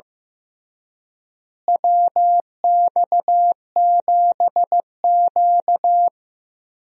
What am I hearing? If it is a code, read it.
Morse code, heard as a keyed tone, about 16 wpm, 700 Hz. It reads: WX7Q
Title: USA Call Signs - 4 Characters - Repeated 3x - 15wpm